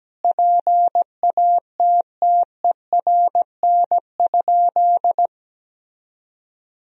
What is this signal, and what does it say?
Morse code, heard as a keyed tone, about 17 wpm, 700 Hz. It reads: PATTERN?
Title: Sentences from Top 1000 Words - Review Part 2 - 17wpm